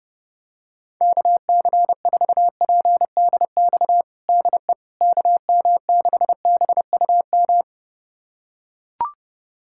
Morse code, heard as a keyed tone, about 32 wpm, 700 Hz. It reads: KC4PDX DE KM6BUM
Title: USA Callsign DE Callsign 30wpm